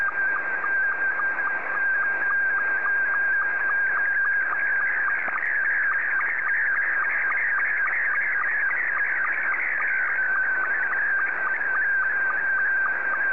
SSTV (slow-scan television), Scottie 2.
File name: sstv / S2-JA7QJQ